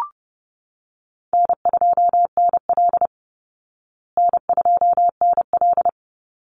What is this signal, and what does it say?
Morse code, heard as a keyed tone, about 30 wpm, 700 Hz. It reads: N2NL N2NL
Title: Call Signs Repeated 3x - 30wpm